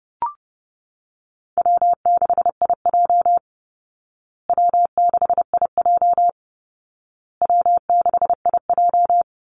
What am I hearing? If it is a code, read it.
Morse code, heard as a keyed tone, about 30 wpm, 700 Hz. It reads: W6IJ W6IJ W6IJ
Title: Call Signs Repeated 2x - 30wpm